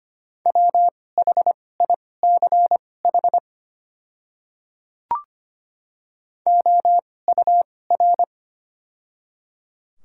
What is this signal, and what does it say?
Morse code, heard as a keyed tone, about 25 wpm, spaced as for 18 wpm, 700 Hz. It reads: WHICH OUR
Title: 2x Character Spacing - Top 200 Words - 25wpm